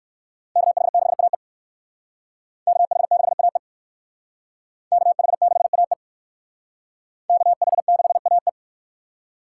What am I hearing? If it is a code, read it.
Morse code, heard as a keyed tone, about 56 wpm, 700 Hz. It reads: KH6AE KH6AE KH6AE KH6AE